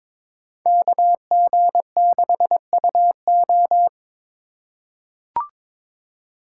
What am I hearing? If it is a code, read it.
Morse code, heard as a keyed tone, about 22 wpm, 700 Hz. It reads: KG6UO
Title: USA Call Signs - 5 Characters - 4x Character Spacing - 22wpm